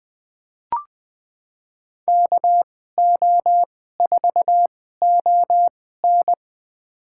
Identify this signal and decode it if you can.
Morse code, heard as a keyed tone, about 20 wpm, spaced as for 15 wpm, 700 Hz. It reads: KO4ON
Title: USA Call Signs - 5 Characters - 2x Character Spacing - 20wpm